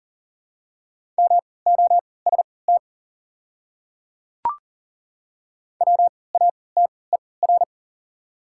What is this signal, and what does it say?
Morse code, heard as a keyed tone, about 40 wpm, spaced as for 23 wpm, 690 Hz. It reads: MOST WATER
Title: 3x Character Spacing - Top 500 Words - 40wpm